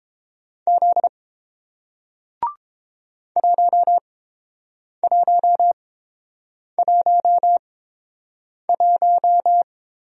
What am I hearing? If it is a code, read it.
Morse code, heard as a keyed tone, about 31 wpm, spaced as for 7 wpm, 700 Hz. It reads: Z 1111